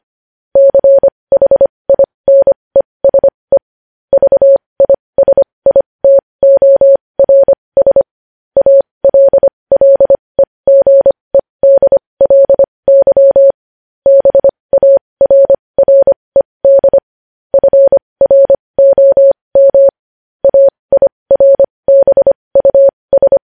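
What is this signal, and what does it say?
Morse code, heard as a keyed tone, about 25 wpm, spaced as for 20 wpm, 550 Hz. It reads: CHINESE VISITORS ALLEGEDLY BARRED FROM AIRBUS
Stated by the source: Morse code news headlines sent at 20.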